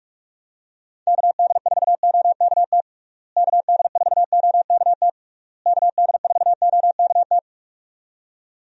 Morse code, heard as a keyed tone, about 46 wpm, 700 Hz. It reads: KD4OKT KD4OKT KD4OKT
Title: USA Call Signs - 6 Characters - Repeated 3x - 45wpm